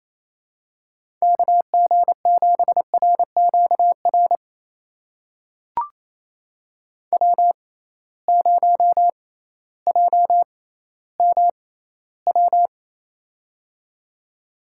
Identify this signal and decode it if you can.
Morse code, heard as a keyed tone, about 28 wpm, spaced as for 15 wpm, 700 Hz. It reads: KG7RQR W0JMW
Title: USA Call Signs - 6x Character Spacing - 28wpm